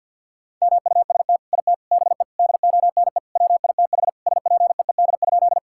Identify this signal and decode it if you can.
Morse code, heard as a keyed tone, about 50 wpm, 700 Hz. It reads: MUST IT BE DONE WITH SPEED?
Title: Sentences from Top 1000 Words - Review Part 5 - 50wpm